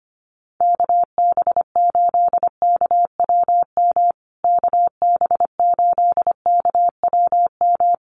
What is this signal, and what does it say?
Morse code, heard as a keyed tone, about 25 wpm, 700 Hz. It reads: KB8KWM KB8KWM
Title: USA Call Signs Repeated 3x 25wpm